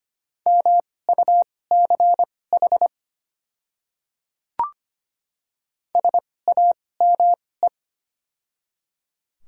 Morse code, heard as a keyed tone, about 25 wpm, spaced as for 18 wpm, 700 Hz. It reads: MUCH SAME
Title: 2x Character Spacing - Top 300 Words - 25wpm